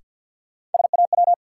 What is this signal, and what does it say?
Morse code, heard as a keyed tone, about 53 wpm, 700 Hz. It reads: SAW
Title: Sentences from Top 300 Words - New Word Drill - 50wpm